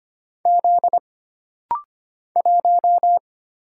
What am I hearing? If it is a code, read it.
Morse code, heard as a keyed tone, about 25 wpm, 700 Hz. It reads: Z 1
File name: Single Character (Letter:Number:Punctuation:Prosign) - Rapid-Fire - 25wpm